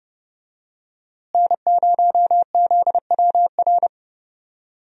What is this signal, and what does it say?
Morse code, heard as a keyed tone, about 29 wpm, 700 Hz. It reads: N0ZWR
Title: USA Call Signs - 5 Characters - Repeated 3x - 30wpm